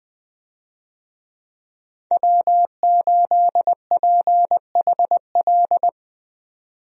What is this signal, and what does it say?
Morse code, heard as a keyed tone, about 20 wpm, 700 Hz. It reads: W8PHL
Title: USA Call Signs - 5 Characters - 20wpm